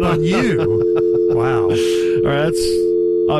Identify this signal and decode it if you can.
Dial tone (North America), 350 + 440 Hz, with speech.
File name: myrtle-10.25.19